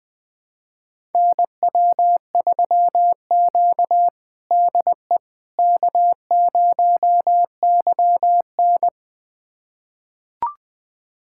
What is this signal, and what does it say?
Morse code, heard as a keyed tone, about 21 wpm, 700 Hz. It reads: NW3Q DE K0YN
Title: USA Callsign DE Callsign - 4 Characters - 20wpm